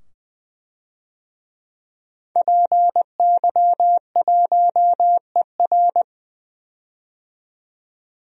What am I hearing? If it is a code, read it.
Morse code, heard as a keyed tone, about 20 wpm, 700 Hz. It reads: PY1ER
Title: DXCC-100 Call Signs - 5 Characters - 20wpm